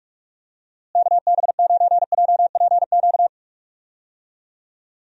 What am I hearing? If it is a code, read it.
Morse code, heard as a keyed tone, about 41 wpm, 700 Hz. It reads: KB9JPQ
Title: USA Call Signs - 6 Characters - Repeated 3x - 45wpm